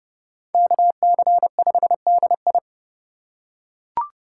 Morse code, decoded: KC5DI

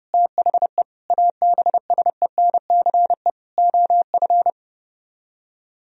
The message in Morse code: THE ABSENCE OF

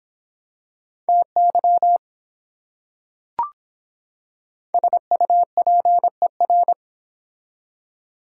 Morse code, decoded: TY SUPER

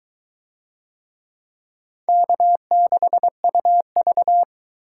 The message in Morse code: K6UV

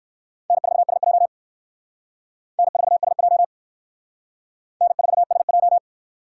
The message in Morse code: N4SJ N4SJ N4SJ

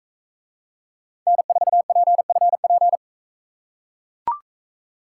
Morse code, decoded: N4PFP

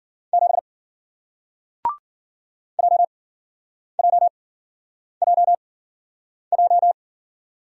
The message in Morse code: 7 JJJJ

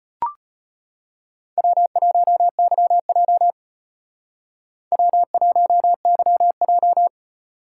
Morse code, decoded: W1YJ W1YJ

